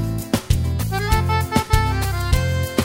There is no code.